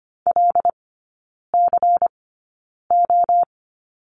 Morse code, heard as a keyed tone, about 25 wpm, spaced as for 9 wpm, 700 Hz. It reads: LCO
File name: course / 25:10wpm / 093 Lesson 16e - Letter c - Three Characters 25wpm